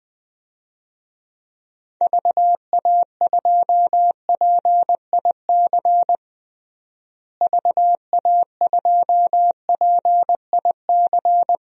Morse code, heard as a keyed tone, about 20 wpm, 700 Hz. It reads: VA2PIC VA2PIC
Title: DXCC-100 Call Signs - 6 Characters - Repeated 3x - 20wpm